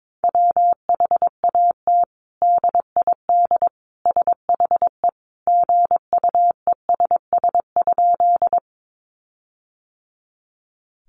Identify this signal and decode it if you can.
Morse code, heard as a keyed tone, about 22 wpm, 700 Hz. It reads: WHAT DID SHE GUESS?
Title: Sentences from Top 1500 Words 22wpm